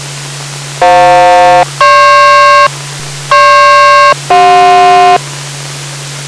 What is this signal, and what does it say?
Aircraft SELCAL, heard as two pairs of tones. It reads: FJ-GP GP-CJ